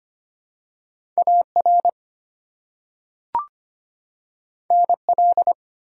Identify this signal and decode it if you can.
Morse code, heard as a keyed tone, about 25 wpm, 700 Hz. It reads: AR NL
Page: morse-code-ninja-practice-25wpm